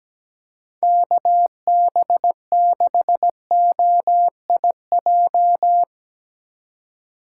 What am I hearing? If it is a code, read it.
Morse code, heard as a keyed tone, about 17 wpm, 700 Hz. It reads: KB6OIJ
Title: USA Call Signs - 6 Characters - 4x Character Spacing - 17wpm